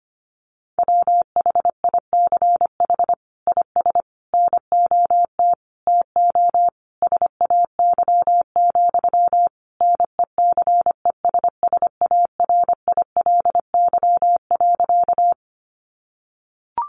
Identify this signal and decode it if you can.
Morse code, heard as a keyed tone, about 25 wpm, 700 Hz. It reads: WHICH IS NOT TO SAY, NECESSARILY.
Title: Sentences from Top 1400 Words 25wpm